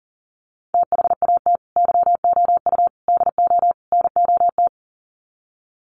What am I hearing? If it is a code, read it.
Morse code, heard as a keyed tone, about 40 wpm, 700 Hz. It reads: THAT YOU DO NOT
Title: Sets of 4 Words 40wpm